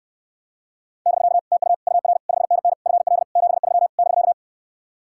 Morse code, heard as a keyed tone, about 67 wpm, 700 Hz. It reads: <BK> TU UR 5NN FL 73 <BK>